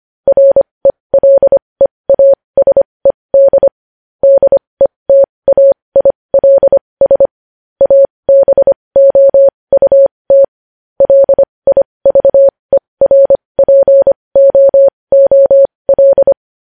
Morse code, heard as a keyed tone, about 25 wpm, spaced as for 20 wpm, 550 Hz. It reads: RELEASED DETAILS ABOUT LIVERPOOL
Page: News Headlines in Morse Code at 20 WPM